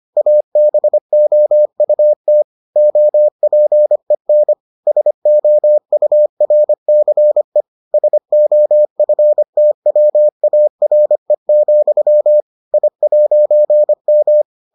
Morse code, heard as a keyed tone, about 25 wpm, 600 Hz. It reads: ABOUT OPEN SOURCE SOFTWARE, I'M